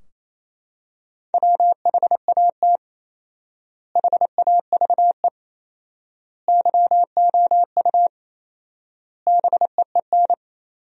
Morse code, heard as a keyed tone, about 28 wpm, 700 Hz. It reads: WHAT HAVE YOU BEEN
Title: 4x Word-Spacing - Sentences from Top 200 Words - 28wpm